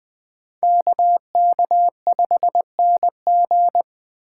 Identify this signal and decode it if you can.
Morse code, heard as a keyed tone, about 20 wpm, 700 Hz. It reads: KK5NG